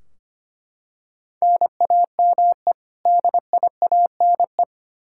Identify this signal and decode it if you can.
Morse code, heard as a keyed tone, about 25 wpm, 700 Hz. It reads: NAME DIANE
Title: K1USN SST QSO Practice 25wpm